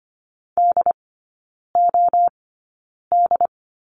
Morse code, heard as a keyed tone, about 25 wpm, spaced as for 9 wpm, 700 Hz. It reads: DOD
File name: Lesson 15e - Number 5 - Three Characters 25wpm